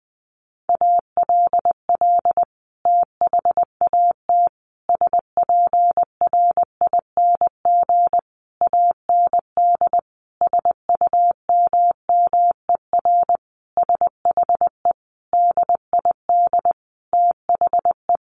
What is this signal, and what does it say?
Morse code, heard as a keyed tone, about 20 wpm, 700 Hz. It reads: ALL THAT SPRING AND SUMMER SHE DID THE
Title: Sentences from Top 1200 Words 20wpm